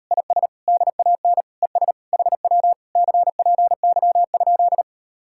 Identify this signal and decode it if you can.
Morse code, heard as a keyed tone, about 39 wpm, 700 Hz. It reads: IS DAN ES HW CPY?